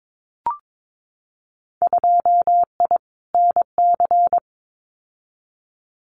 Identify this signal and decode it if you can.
Morse code, heard as a keyed tone, about 22 wpm, 700 Hz. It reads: 2I NC
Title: Winter Field Day Exchanges 22wpm